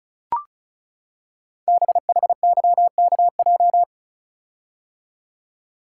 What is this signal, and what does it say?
Morse code, heard as a keyed tone, about 36 wpm, 700 Hz. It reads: BHYKJ
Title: Code Groups - 5 Letters - 35wpm